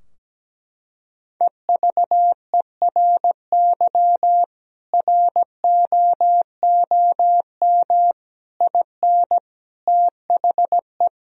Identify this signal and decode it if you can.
Morse code, heard as a keyed tone, about 17 wpm, 700 Hz. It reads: EVERY ROOM IN THE
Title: Sentences from Top 1000 Words - Review Part 3 - 17wpm